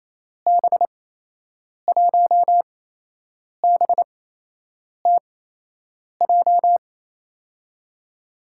Morse code, heard as a keyed tone, about 28 wpm, spaced as for 8 wpm, 700 Hz. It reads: B1BTJ
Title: USA Call Signs - 6 Characters - 8x Character Spacing - 28wpm